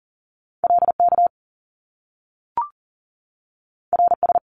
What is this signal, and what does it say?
Morse code, heard as a keyed tone, about 41 wpm, 700 Hz. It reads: LK RS
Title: Top Two Letter Combinations in Words 40wpm